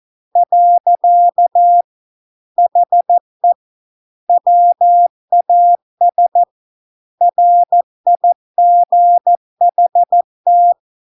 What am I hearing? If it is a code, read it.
Morse code, heard as a keyed tone, about 14 wpm, 700 Hz. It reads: . HE WAS RIGHT